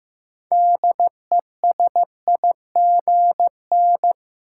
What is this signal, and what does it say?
Morse code, heard as a keyed tone, about 15 wpm, 700 Hz. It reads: DESIGN
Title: Sentences from Top 800 Words - New Word Drill - 15wpm